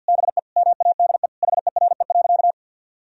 Morse code, heard as a keyed tone, about 52 wpm, 700 Hz. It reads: BE MADE HERE.
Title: Sentences from Top 1000 Words - Review Part 4 - 50wpm